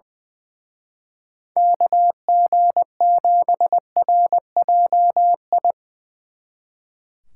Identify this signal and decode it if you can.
Morse code, heard as a keyed tone, about 22 wpm, 700 Hz. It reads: KG7RJI
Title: Speed Racing - USA 2x3 Call Signs 20wpm